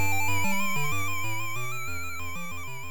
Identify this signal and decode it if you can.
There is no code.